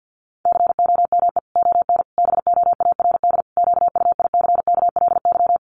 Morse code, heard as a keyed tone, about 50 wpm, 700 Hz. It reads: COME ON BOARD QUICKLY
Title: Sentences from Top 1300 Words 50wpm